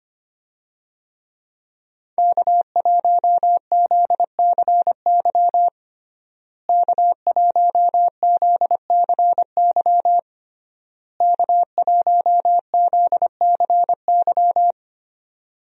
Morse code, heard as a keyed tone, about 25 wpm, 700 Hz. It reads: K1ZCY K1ZCY K1ZCY
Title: USA Call Signs - 5 Characters - Repeated 3x - 25wpm